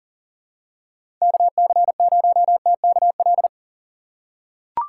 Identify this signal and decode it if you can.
Morse code, heard as a keyed tone, about 40 wpm, 700 Hz. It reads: KC0TKL
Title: USA Call Signs - 6 Characters - 6x Character Spacing - 40wpm